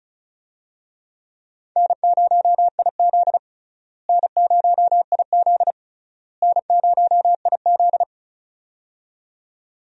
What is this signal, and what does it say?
Morse code, heard as a keyed tone, about 36 wpm, 700 Hz. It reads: N0IZ N0IZ N0IZ
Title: USA Call Signs - 4 Characters - Repeated 3x - 35wpm